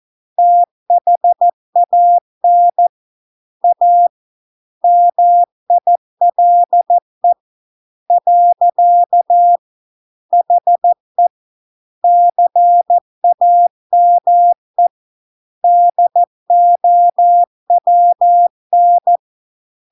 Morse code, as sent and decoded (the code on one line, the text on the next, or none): - .... .- -. / .- / -- .. .-.. / .-.-.- / .... / -.-. .- -- . / -.. --- .-- -.
THAN A MILE . HE CAME DOWN